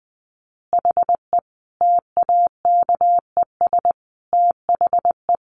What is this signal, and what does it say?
Morse code, heard as a keyed tone, about 20 wpm, 700 Hz. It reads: HE TAKES THE
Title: Additional Easy Phrases 20wpm